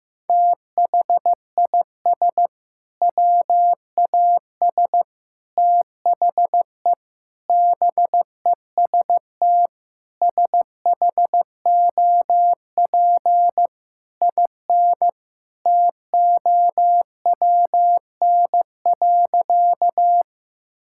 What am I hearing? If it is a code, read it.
Morse code, heard as a keyed tone, about 15 wpm, 700 Hz. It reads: THIS WAS THE BEST SHOP IN TOWN.
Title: Sentences from Top 1600 Words 15wpm